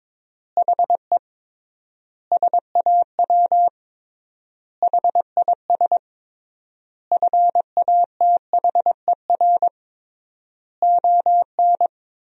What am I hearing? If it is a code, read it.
Morse code, heard as a keyed tone, about 22 wpm, 700 Hz. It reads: HE SAW HIS FATHER ON
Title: 3x Word-Spacing - Sentences from Top 500 Words - 22wpm